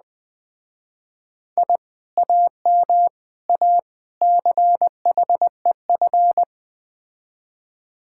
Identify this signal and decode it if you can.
Morse code, heard as a keyed tone, about 20 wpm, 700 Hz. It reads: I AM A CHEF
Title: Speed-Racing Common QSO Phrases - 17wpm